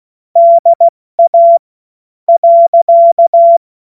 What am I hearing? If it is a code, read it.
Morse code, heard as a keyed tone, about 16 wpm, spaced as for 11 wpm, 680 Hz. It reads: DA.
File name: Morsekurs / Witze / Witz 21.November 2023 70 F80